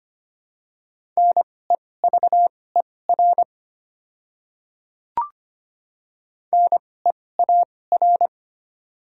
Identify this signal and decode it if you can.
Morse code, heard as a keyed tone, about 25 wpm, spaced as for 18 wpm, 700 Hz. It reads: NEVER NEAR
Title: 2x Character Spacing - Top 300 Words - 25wpm